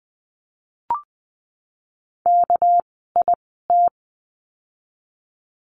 Morse code, heard as a keyed tone, about 20 wpm, spaced as for 15 wpm, 700 Hz. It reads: KIT
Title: 199 Lesson 32f - Letter k - Words 20wpm